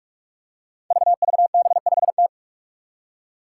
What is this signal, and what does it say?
Morse code, heard as a keyed tone, about 41 wpm, 700 Hz. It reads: VVB5T
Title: Five Challenging Letter-Number Combinations 45wpm